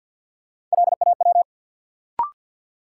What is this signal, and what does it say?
Morse code, heard as a keyed tone, about 52 wpm, 700 Hz. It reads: LAW
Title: Sentences from Top 600 Words - New Word Drill - 50wpm